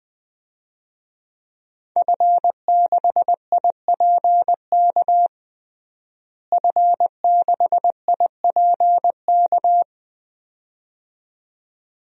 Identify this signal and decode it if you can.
Morse code, heard as a keyed tone, about 20 wpm, 700 Hz. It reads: F6IPK F6IPK
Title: DXCC-100 Call Signs - 5 Characters - Repeated 2x - 20wpm